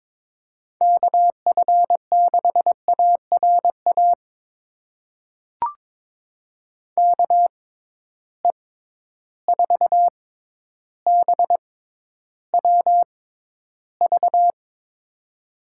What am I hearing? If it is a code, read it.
Morse code, heard as a keyed tone, about 21 wpm, spaced as for 11 wpm, 700 Hz. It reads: KF6ARA KE4BWV
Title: USA Call Signs - 6 Characters - 6x Character Spacing - 22wpm